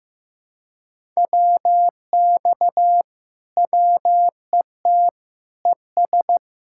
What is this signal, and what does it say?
Morse code, heard as a keyed tone, about 15 wpm, 700 Hz. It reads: WX WET ES
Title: Speed-Racing Common QSO Phrases Encore - 15wpm